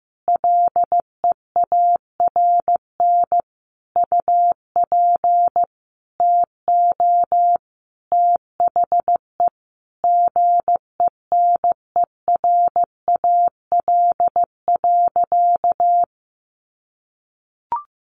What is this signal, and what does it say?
Morse code, heard as a keyed tone, about 15 wpm, 700 Hz. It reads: LEARN UP TO THE GENERAL.
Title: Sentences from Top 1500 Words 15wpm